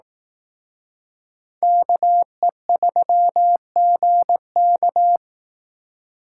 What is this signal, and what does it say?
Morse code, heard as a keyed tone, about 18 wpm, 700 Hz. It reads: KE3GK
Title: Speed Racing - USA Call Signs 15wpm